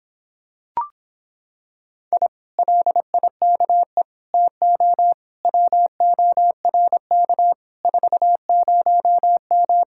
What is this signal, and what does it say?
Morse code, heard as a keyed tone, about 26 wpm, 700 Hz. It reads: I LIKE TO WORK 40M